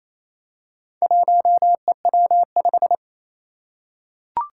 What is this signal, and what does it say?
Morse code, heard as a keyed tone, about 29 wpm, 700 Hz. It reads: 1EW5
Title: Four Challenging Letter-Number Combinations 28wpm